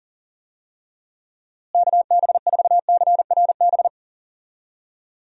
Morse code, heard as a keyed tone, about 40 wpm, 700 Hz. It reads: KB4CRB
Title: USA Call Signs - 6 Characters - Repeated 2x - 40wpm